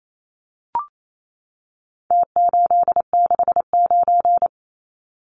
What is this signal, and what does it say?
Morse code, heard as a keyed tone, about 28 wpm, 700 Hz. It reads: T869
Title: Numbers - Contest Style - 4-Digit 28wpm